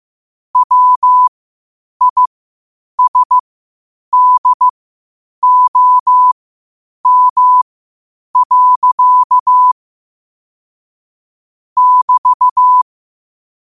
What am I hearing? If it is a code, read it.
Morse code, heard as a keyed tone, about 15 wpm, spaced as for 8 wpm, 1000 Hz. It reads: WISDOM. <BT>